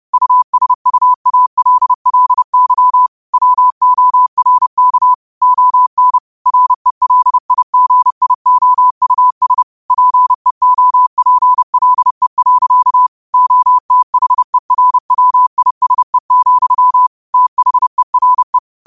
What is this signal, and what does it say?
Morse code, heard as a keyed tone, about 30 wpm, 1000 Hz. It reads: USUALLY WORK ON RELIGIOUS PEOPLE. OTHERWISE, THERE
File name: qotdfiles / 30wpm